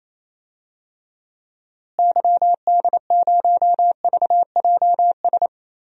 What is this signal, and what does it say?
Morse code, heard as a keyed tone, about 28 wpm, 700 Hz. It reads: YD0VJS
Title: DXCC-100 Call Signs - 6 Characters - Repeated 2x - 28wpm